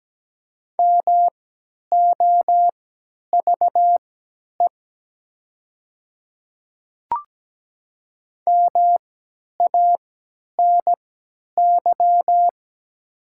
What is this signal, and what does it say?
Morse code, heard as a keyed tone, about 17 wpm, spaced as for 10 wpm, 700 Hz. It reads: MOVE MANY
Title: 3x Character Spacing - Top 200 Words - 17wpm